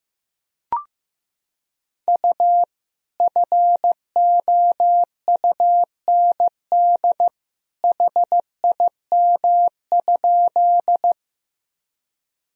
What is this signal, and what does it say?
Morse code, heard as a keyed tone, about 15 wpm, 700 Hz. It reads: U FOUND HIM?
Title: Sentences from Top 300 Words and Abbreviations 15wpm